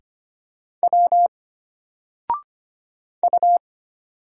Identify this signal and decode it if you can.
Morse code, heard as a keyed tone, about 25 wpm, 700 Hz. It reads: W U
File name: Single-Letter-Number-25wpm